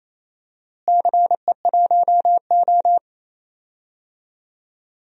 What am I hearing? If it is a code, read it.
Morse code, heard as a keyed tone, about 28 wpm, 700 Hz. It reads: CE1O
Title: DXCC-100 Call Signs - 4 Characters - Repeated 3x - 28wpm